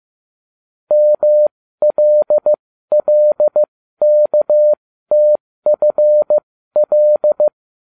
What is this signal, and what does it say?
Morse code, heard as a keyed tone, about 15 wpm, spaced as for 13 wpm, 600 Hz. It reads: MLLKTFL
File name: cw / ubacw20 3